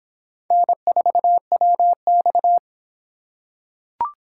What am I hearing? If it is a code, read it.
Morse code, heard as a keyed tone, about 24 wpm, 700 Hz. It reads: N4WX